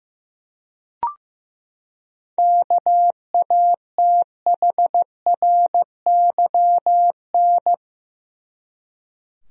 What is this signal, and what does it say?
Morse code, heard as a keyed tone, about 15 wpm, 700 Hz. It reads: KATHRYN
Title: USA Common Names - 15wpm